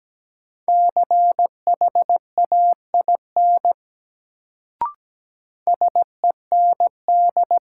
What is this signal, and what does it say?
Morse code, heard as a keyed tone, about 17 wpm, 700 Hz. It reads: CHAIN SEND
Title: Sentences from Top 1500 Words - New Word Drill - 17wpm